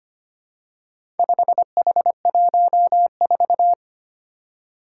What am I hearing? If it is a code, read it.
Morse code, heard as a keyed tone, about 25 wpm, 700 Hz. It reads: 5H14
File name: practice / 25wpm / Four Letter-Number Combinations - 25wpm